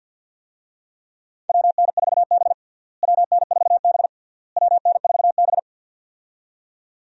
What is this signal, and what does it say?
Morse code, heard as a keyed tone, about 52 wpm, 700 Hz. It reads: WN4B WN4B WN4B
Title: USA Call Signs - 4 Characters - Repeated 3x - 50wpm